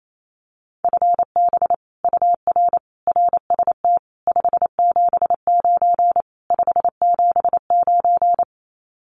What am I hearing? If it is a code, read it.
Morse code, heard as a keyed tone, about 28 wpm, 700 Hz. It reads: FB UR RST 579 579
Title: Common QSO Phrases Encore 28wpm